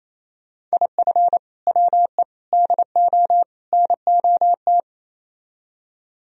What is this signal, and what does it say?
Morse code, heard as a keyed tone, about 36 wpm, 700 Hz. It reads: IF WE DO NOT